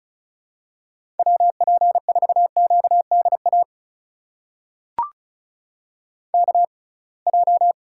Morse code, heard as a keyed tone, about 36 wpm, spaced as for 27 wpm, 700 Hz. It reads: WP4QDA KJ
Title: USA Call Signs - 6 Characters - 6x Character Spacing - 35wpm